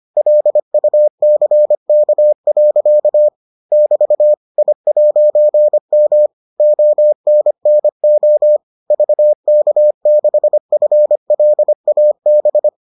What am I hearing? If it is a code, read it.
Morse code, heard as a keyed tone, about 25 wpm, 600 Hz. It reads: LUCK. <BT> I'M ONNO VK6FLAB